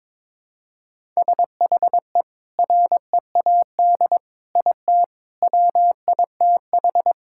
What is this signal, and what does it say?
Morse code, heard as a keyed tone, about 22 wpm, 700 Hz. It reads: SHE READ IT WITH
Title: Sentences from Top 1000 Words - Review Part 6 - 22wpm